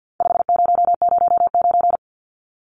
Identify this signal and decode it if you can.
Morse code, heard as a keyed tone, about 50 wpm, 700 Hz. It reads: 5009
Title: Numbers - 4-Digit 50wpm